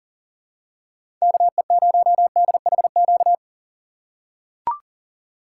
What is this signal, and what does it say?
Morse code, heard as a keyed tone, about 40 wpm, 700 Hz. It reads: KE0DHQ
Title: USA Call Signs - 6 Characters - 6x Character Spacing - 40wpm